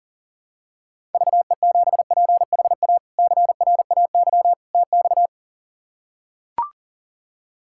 Morse code, heard as a keyed tone, about 40 wpm, 700 Hz. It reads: VE7PHA CRAY TX
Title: K1USN SST QSO Practice - 40wpm